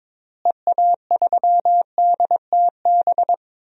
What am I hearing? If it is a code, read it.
Morse code, heard as a keyed tone, about 22 wpm, 700 Hz. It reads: EA3DTB
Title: DXCC-100 Call Signs - 6 Characters - Repeated 2x - 22wpm